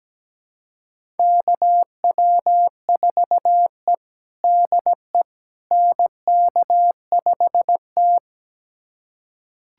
Morse code, heard as a keyed tone, about 17 wpm, 700 Hz. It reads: KW4E DE NK5T
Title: USA Callsign DE Callsign - 4 Characters - 17wpm